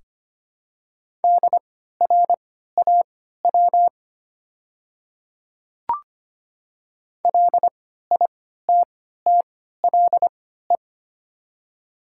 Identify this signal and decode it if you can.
Morse code, heard as a keyed tone, about 25 wpm, spaced as for 14 wpm, 700 Hz. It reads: DRAW LITTLE
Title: 3x Character Spacing - Top 500 Words - 25wpm